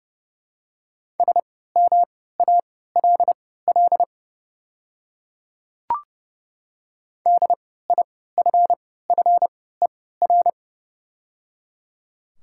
Morse code, heard as a keyed tone, about 31 wpm, spaced as for 17 wpm, 700 Hz. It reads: SMALL DIFFER
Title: 3x Character Spacing - Top 300 Words - 30wpm